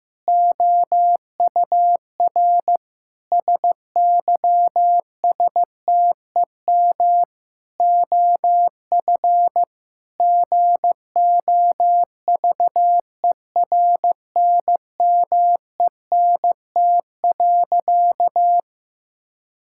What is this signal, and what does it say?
Morse code, heard as a keyed tone, about 15 wpm, 700 Hz. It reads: OUR SYSTEM OF GOVERNMENT.
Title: Sentences from Top 1000 Words - Review Part 6 - 15wpm